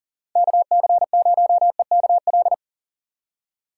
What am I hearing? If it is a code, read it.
Morse code, heard as a keyed tone, about 39 wpm, 690 Hz. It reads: KC0EKL
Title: USA Call Signs - 4x Character Spacing - 40wpm